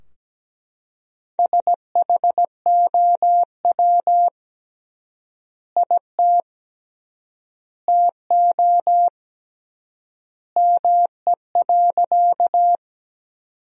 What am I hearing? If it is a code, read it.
Morse code, heard as a keyed tone, about 17 wpm, 700 Hz. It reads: SHOW IT TO ME.
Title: 3x Word-Spacing - Sentences from Top 200 Words - 17wpm